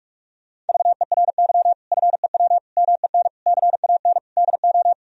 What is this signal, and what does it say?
Morse code, heard as a keyed tone, about 41 wpm, 700 Hz. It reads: VERY FEW MEN CAN DO
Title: Sentences from Top 1000 Words - Review Part 1 - 45wpm